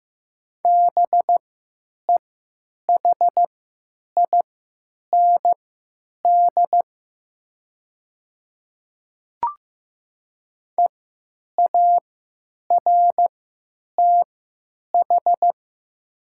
Morse code, heard as a keyed tone, about 15 wpm, spaced as for 9 wpm, 700 Hz. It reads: BEHIND EARTH
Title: 3x Character Spacing - Top 500 Words - 15wpm